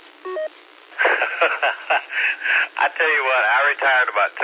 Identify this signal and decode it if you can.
Emergency Alert System (SAME) tones, heard as harsh buzzing bursts, with speech. It is heard from 0.9 to 4.1 s.